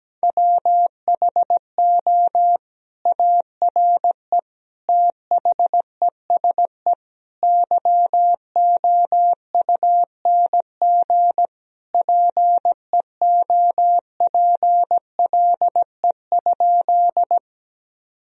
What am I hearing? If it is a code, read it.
Morse code, heard as a keyed tone, about 17 wpm, 700 Hz. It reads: WHO ARE THESE YOUNG PEOPLE?
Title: Sentences from Top 1000 Words - Review Part 1 - 17wpm